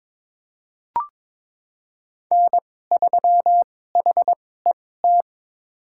Morse code, heard as a keyed tone, about 22 wpm, spaced as for 16 wpm, 700 Hz. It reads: N3HET
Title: USA Call Signs - 5 Characters - 2x Character Spacing - 22wpm